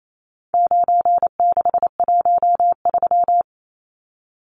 Morse code, heard as a keyed tone, about 28 wpm, 700 Hz. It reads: 9613
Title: Numbers - 4-Digit 28wpm